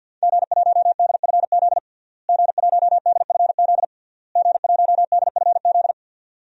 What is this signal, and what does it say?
Morse code, heard as a keyed tone, about 48 wpm, 700 Hz. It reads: G1DFZ G1DFZ G1DFZ
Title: DXCC-100 Call Signs - 5 Characters - Repeated 3x - 50wpm